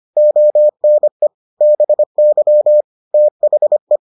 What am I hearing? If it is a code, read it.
Morse code, heard as a keyed tone, about 25 wpm, 600 Hz. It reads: ONE BY THE